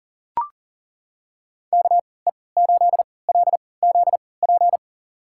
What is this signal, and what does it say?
Morse code, heard as a keyed tone, about 40 wpm, spaced as for 23 wpm, 700 Hz. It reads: KE8LZP
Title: USA Call Signs - 6 Characters - 3x Character Spacing - 40wpm